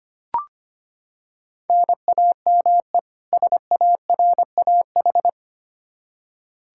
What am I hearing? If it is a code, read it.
Morse code, heard as a keyed tone, about 25 wpm, 700 Hz. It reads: NAME SARAH
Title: K1USN SST QSO Practice 25wpm